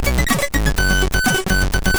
There is no code.